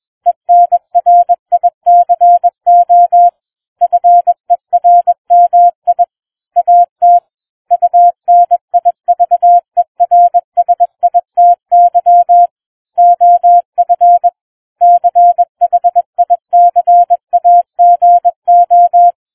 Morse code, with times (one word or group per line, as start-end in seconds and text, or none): 0.3-3.3 ENRICO
3.8-6.0 FERMI
6.6-7.2 AT
7.7-12.5 UNIVERSITY
13.0-14.3 OF
14.8-19.1 CHICAGO